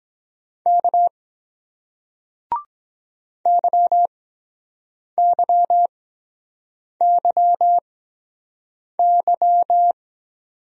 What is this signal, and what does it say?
Morse code, heard as a keyed tone, about 23 wpm, spaced as for 7 wpm, 700 Hz. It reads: K YYYY